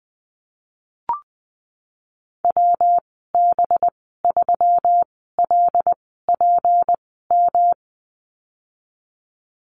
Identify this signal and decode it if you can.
Morse code, heard as a keyed tone, about 19 wpm, spaced as for 14 wpm, 700 Hz. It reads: WB3LPM
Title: 186 Lesson 30g - Letter b - Call Signs 20wpm